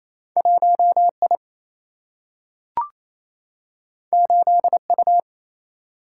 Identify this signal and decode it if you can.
Morse code, heard as a keyed tone, about 28 wpm, 700 Hz. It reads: 1 I 8 U
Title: Single-Letter Single-Number 28wpm